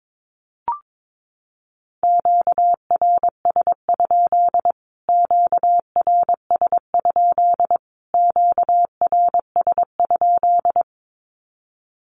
Morse code, heard as a keyed tone, about 22 wpm, 700 Hz. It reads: QRS? QRS? QRS?
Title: Q-Codes Repeated 3x 22wpm